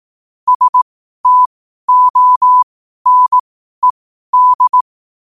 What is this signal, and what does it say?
Morse code, heard as a keyed tone, about 18 wpm, spaced as for 12 wpm, 1000 Hz. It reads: STONED
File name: qotdfiles / 12wpm